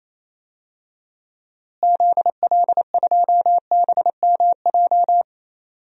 Morse code, heard as a keyed tone, about 28 wpm, 700 Hz. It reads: ZL2BMJ
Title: DXCC-100 Call Signs - 6 Characters - Repeated 3x - 28wpm